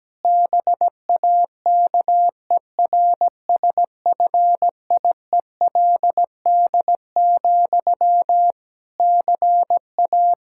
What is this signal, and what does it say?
Morse code, heard as a keyed tone, about 17 wpm, 700 Hz. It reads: BAKERSFIELD, CA
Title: K1USN SST QSO Practice 17wpm